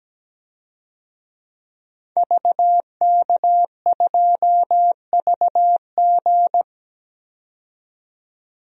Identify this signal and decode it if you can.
Morse code, heard as a keyed tone, about 17 wpm, 700 Hz. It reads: VK2VG
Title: DXCC-100 Call Signs - 5 Characters - 17wpm